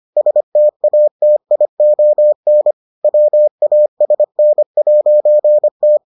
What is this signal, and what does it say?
Morse code, heard as a keyed tone, about 25 wpm, 600 Hz. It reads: STATION WASN'T